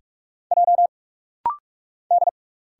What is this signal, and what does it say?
Morse code, heard as a keyed tone, about 46 wpm, 700 Hz. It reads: J D
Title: Single Letter-Number Alt - Rapid-Fire 45wpm